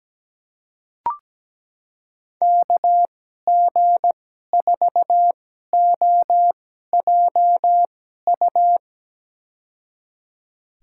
Morse code, heard as a keyed tone, about 17 wpm, spaced as for 12 wpm, 700 Hz. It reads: KG4OJU
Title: USA Call Signs - 6 Characters - 2x Character Spacing - 17wpm